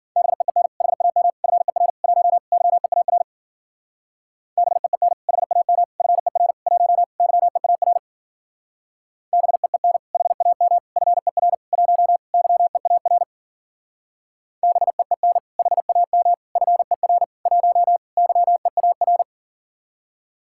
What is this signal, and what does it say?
Morse code, heard as a keyed tone, about 47 wpm, 700 Hz. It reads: BEEN HAM FER 1 YEAR BEEN HAM FER 1 YEAR BEEN HAM FER 1 YEAR BEEN HAM FER 1 YEAR
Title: Speed-Racing Common QSO Phrases - 40wpm